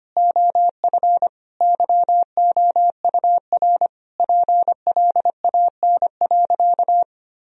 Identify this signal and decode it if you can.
Morse code, heard as a keyed tone, about 25 wpm, 700 Hz. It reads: OF YOUR PLAN.
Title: Sentences from Top 1000 Words - Review Part 2 - 25wpm